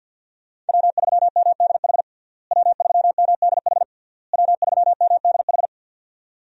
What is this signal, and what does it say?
Morse code, heard as a keyed tone, about 52 wpm, 700 Hz. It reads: W3MDH W3MDH W3MDH
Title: USA Call Signs - 5 Characters - Repeated 3x - 50wpm